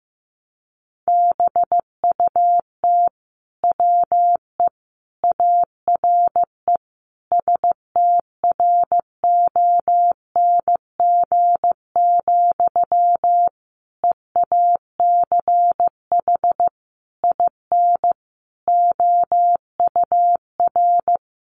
Morse code, heard as a keyed tone, about 15 wpm, 700 Hz. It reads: BUT WE ARE STRONG, EACH IN OUR
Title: Sentences from Top 800 Words 15wpm